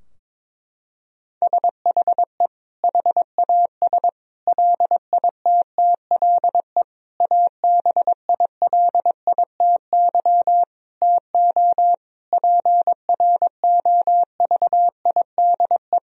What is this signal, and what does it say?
Morse code, heard as a keyed tone, about 22 wpm, 700 Hz. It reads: SHE HAS LITTLE ABILITY TO PROVIDE
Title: Sentences from Top 1000 Words - Review Part 3 - 22wpm